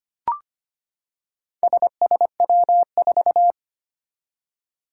Morse code, heard as a keyed tone, about 25 wpm, 700 Hz. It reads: SSW4
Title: Four Challenging Letter-Number Combinations 25wpm